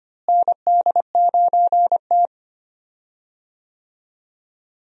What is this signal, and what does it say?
Morse code, heard as a keyed tone, about 25 wpm, 700 Hz. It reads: ND9T
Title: USA Call Signs - 4 Characters - Repeated 2x - 25wpm